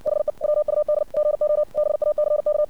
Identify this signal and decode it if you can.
Morse code, heard as a keyed tone, about 60 wpm, 600 Hz. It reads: HE WUD GO BACK
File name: archived / CRSnet